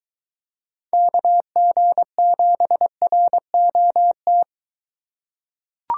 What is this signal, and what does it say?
Morse code, heard as a keyed tone, about 22 wpm, 700 Hz. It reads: KG7ROT